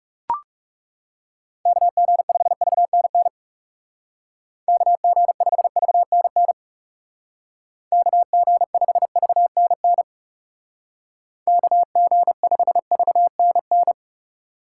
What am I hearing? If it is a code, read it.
Morse code, heard as a keyed tone, about 39 wpm, 700 Hz. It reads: KG5VNN KG5VNN KG5VNN KG5VNN